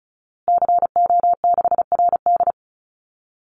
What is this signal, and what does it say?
Morse code, heard as a keyed tone, about 34 wpm, 700 Hz. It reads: CO6RD
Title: Call Signs Repeated 2x - 35wpm